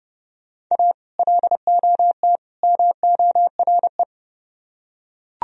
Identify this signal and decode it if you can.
Morse code, heard as a keyed tone, about 31 wpm, 700 Hz. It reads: A LOT MORE